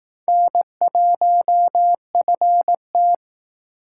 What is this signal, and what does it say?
Morse code, heard as a keyed tone, about 18 wpm, 700 Hz. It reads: N1FT
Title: Speed Racing - USA 1x2 Call Signs 15wpm